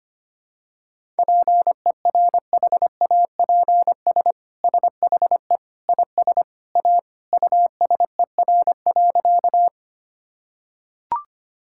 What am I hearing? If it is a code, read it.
Morse code, heard as a keyed tone, about 25 wpm, 700 Hz. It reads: PERHAPS SHE IS A USER.
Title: Sentences from Top 1000 Words - Review Part 5 - 25wpm